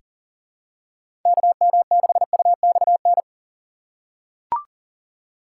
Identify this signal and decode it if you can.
Morse code, heard as a keyed tone, about 39 wpm, 700 Hz. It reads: KM6UXN
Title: USA Call Signs - 6 Characters - 4x Character Spacing - 40wpm